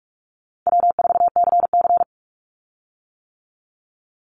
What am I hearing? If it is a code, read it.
Morse code, heard as a keyed tone, about 44 wpm, 700 Hz. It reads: W4CC
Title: Call Signs - 45wpm